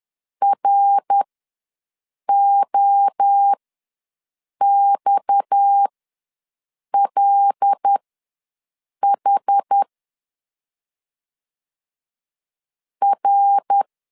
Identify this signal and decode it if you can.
Morse code, heard as a keyed tone, about 10 wpm, spaced as for 6 wpm, 800 Hz. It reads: ROXLH R
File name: LEZIONI / 6 PKX0 / 6 PKX0 ws 10wpm 4220